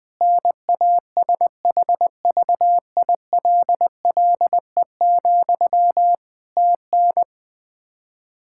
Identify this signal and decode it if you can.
Morse code, heard as a keyed tone, about 21 wpm, 700 Hz. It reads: NASHVILLE, TN
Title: Practice QSO - Exchange 1 - Beginning - 20wpm